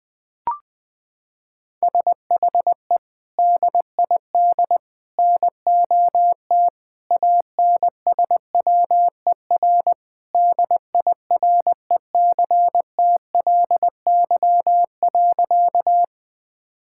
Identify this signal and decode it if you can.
Morse code, heard as a keyed tone, about 20 wpm, 700 Hz. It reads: SHE DID NOT ANSWER DIRECTLY.
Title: Sentences from Top 1000 Words - Review Part 5 - 20wpm